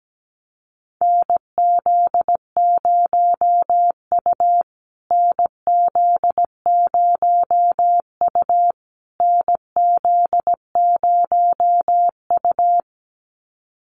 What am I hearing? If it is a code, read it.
Morse code, heard as a keyed tone, about 17 wpm, 700 Hz. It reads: NZ0U NZ0U NZ0U
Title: USA Call Signs Repeated 3x 17wpm